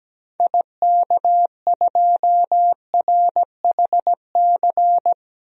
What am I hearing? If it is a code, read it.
Morse code, heard as a keyed tone, about 17 wpm, 700 Hz. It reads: IK2RHC
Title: DXCC-100 Call Signs - 6 Characters - Repeated 2x - 17wpm